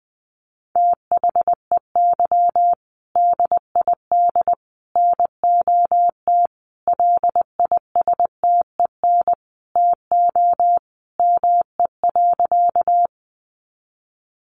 Morse code, heard as a keyed tone, about 19 wpm, 700 Hz. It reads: THEY DID NOT LISTEN TO ME.
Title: Sentences from Top 600 Words 20wpm